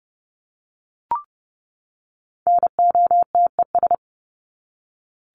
Morse code, heard as a keyed tone, about 30 wpm, 700 Hz. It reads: NOTES
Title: ES:S Suffix - Top 100 Words with Base Words 30wpm